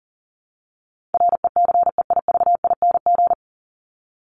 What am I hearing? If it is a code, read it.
Morse code, heard as a keyed tone, about 40 wpm, 700 Hz. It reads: RECEIVING
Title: ING Suffix - Top 100 Words with Base Words 40wpm